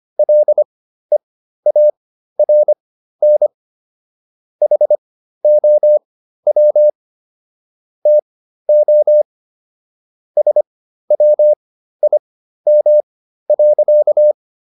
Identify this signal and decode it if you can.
Morse code, heard as a keyed tone, about 25 wpm, spaced as for 13 wpm, 600 Hz. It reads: LEARN HOW TO SWIM.